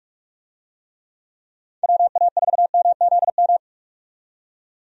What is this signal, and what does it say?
Morse code, heard as a keyed tone, about 44 wpm, 700 Hz. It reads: WA4MZM
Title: USA Call Signs - 6 Characters - 45wpm